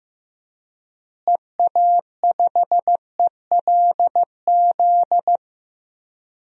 Morse code, heard as a keyed tone, about 15 wpm, 700 Hz. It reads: EA5ELZ
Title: DXCC-100 Call Signs - 6 Characters - Repeated 3x - 15wpm